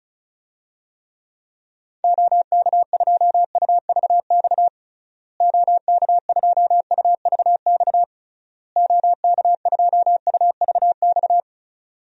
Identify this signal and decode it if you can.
Morse code, heard as a keyed tone, about 36 wpm, 700 Hz. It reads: OK2UVX OK2UVX OK2UVX
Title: DXCC-100 Call Signs - 6 Characters - Repeated 3x - 35wpm